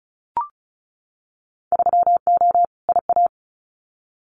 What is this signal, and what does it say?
Morse code, heard as a keyed tone, about 36 wpm, 690 Hz. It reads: 3O IA
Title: Winter Field Day Exchanges 35wpm